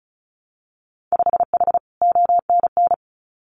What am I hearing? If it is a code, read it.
Morse code, heard as a keyed tone, about 36 wpm, 700 Hz. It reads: 5H ONN
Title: Winter Field Day Exchanges Encore 35wpm